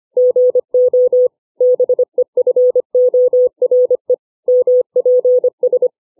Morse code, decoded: GO BEFORE MPS